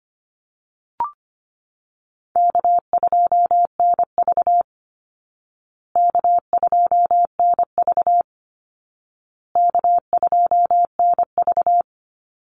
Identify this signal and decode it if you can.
Morse code, heard as a keyed tone, about 25 wpm, 700 Hz. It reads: K2NV K2NV K2NV
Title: Call Signs Repeated 2x - 25wpm